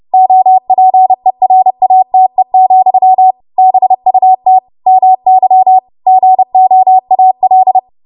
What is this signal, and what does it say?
Morse code, heard as a keyed tone, about 30 wpm, 750 Hz. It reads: OPERATE, BUT MY GOAL
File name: Morse / 30WPM